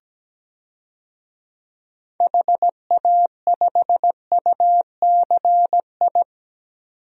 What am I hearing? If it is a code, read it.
Morse code, heard as a keyed tone, about 17 wpm, 700 Hz. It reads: HA5UCI
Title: DXCC-100 Call Signs - 6 Characters - 17wpm